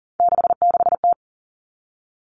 Morse code, heard as a keyed tone, about 40 wpm, 700 Hz. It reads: 66T
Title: Common Cut Numbers - 3-Digit 40wpm